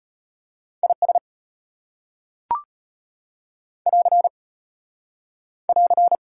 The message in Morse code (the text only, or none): IS <AR> <AR>